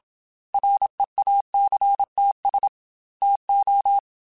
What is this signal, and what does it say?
Morse code, heard as a keyed tone, about 27 wpm, 800 Hz. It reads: REACTS TO
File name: Technical Subjects / Technical Subjects-mp3s / Reactance @25